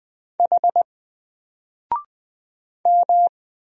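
Morse code, heard as a keyed tone, about 20 wpm, 700 Hz. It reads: H M
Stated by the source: Single letter and number Morse drill at 20 wpm, from N4LCD.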